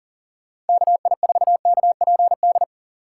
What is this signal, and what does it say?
Morse code, heard as a keyed tone, about 40 wpm, 700 Hz. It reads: KI4KPD
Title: USA Call Signs - 6 Characters - 3x Character Spacing - 40wpm